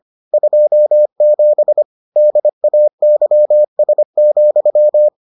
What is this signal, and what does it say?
Morse code, heard as a keyed tone, about 25 wpm, 600 Hz. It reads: 27 DAYS,